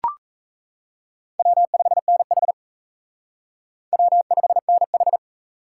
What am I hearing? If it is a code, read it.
Morse code, heard as a keyed tone, about 40 wpm, 700 Hz. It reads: W5NH W5NH